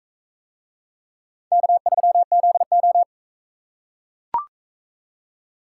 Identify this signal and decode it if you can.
Morse code, heard as a keyed tone, about 44 wpm, 700 Hz. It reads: K3ZO